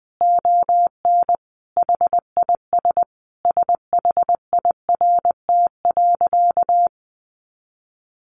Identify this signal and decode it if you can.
Morse code, heard as a keyed tone, about 19 wpm, 700 Hz. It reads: ON HIS SHIRT.
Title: Additional Easy Phrases 20wpm